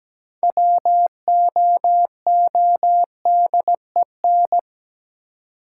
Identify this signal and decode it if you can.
Morse code, heard as a keyed tone, about 17 wpm, 700 Hz. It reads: WOODEN
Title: Six Letter Words - 17wpm